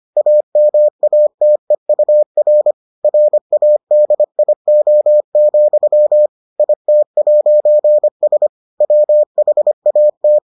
Morse code, heard as a keyed tone, about 25 wpm, 600 Hz. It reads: AMATEUR RADIO, IT'S WHAT